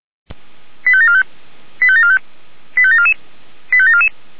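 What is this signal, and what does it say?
Five-tone selective call (ZVEI), heard as quick runs of five tones, tones of about 70 ms. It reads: 85745 85745 85744 85744
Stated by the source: ZVEI five-tone selective call received on an SDR.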